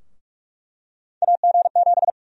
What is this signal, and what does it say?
Morse code, heard as a keyed tone, about 46 wpm, 700 Hz. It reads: AG7
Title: USA Call Sign Prefixes - Repeated 2x - 45wpm